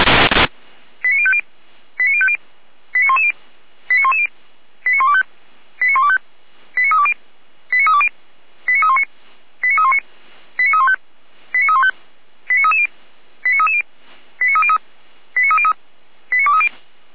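Five-tone selective call (ZVEI), heard as quick runs of five tones, tones of about 70 ms. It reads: 89050 89050 89110 89110 89125 89125 89320 89320 89319 89319 89315 89315 89330 89330 89303 89303 89230
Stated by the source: ZVEI five-tone selective call received on SDR.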